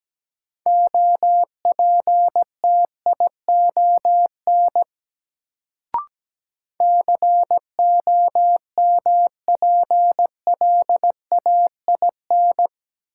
Morse code, heard as a keyed tone, about 17 wpm, 700 Hz. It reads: OPTION COMPLAIN
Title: Sentences from Top 1300 Words - New Word Drill Alt - 17wpm